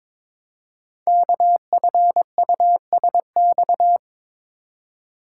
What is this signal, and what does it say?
Morse code, heard as a keyed tone, about 22 wpm, 700 Hz. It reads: KFUSX
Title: Code Groups - 5 Letters - 22wpm